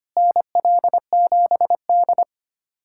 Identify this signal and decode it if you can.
Morse code, heard as a keyed tone, about 25 wpm, 700 Hz. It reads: NL7D
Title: USA Callsign DE Callsign - 4 Characters - 25wpm